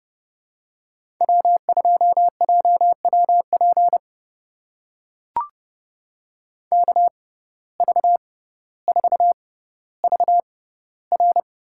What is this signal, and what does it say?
Morse code, heard as a keyed tone, about 30 wpm, spaced as for 15 wpm, 700 Hz. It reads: W2JWP KV4VR
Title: USA Call Signs - 5 Characters - 6x Character Spacing - 30wpm